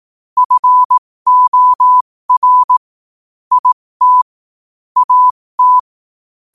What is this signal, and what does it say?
Morse code, heard as a keyed tone, about 18 wpm, spaced as for 15 wpm, 1000 Hz. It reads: FOR IT AT
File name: qotdfiles / 15wpm